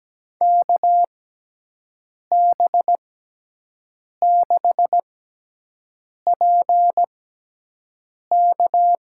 Morse code, heard as a keyed tone, about 17 wpm, spaced as for 6 wpm, 700 Hz. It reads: KB6PK
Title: USA Call Signs - 6 Characters - 6x Character Spacing - 17wpm